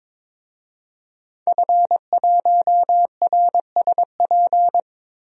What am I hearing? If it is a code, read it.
Morse code, heard as a keyed tone, about 22 wpm, 700 Hz. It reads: F1RSP
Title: DXCC-100 Call Signs - 5 Characters - 22wpm